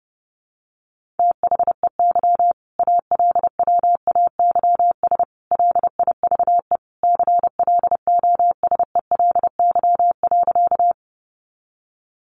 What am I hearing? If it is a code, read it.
Morse code, heard as a keyed tone, about 30 wpm, 700 Hz. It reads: THEY ALWAYS LIVE CLOSELY.
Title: Sentences from Top 1400 Words 30wpm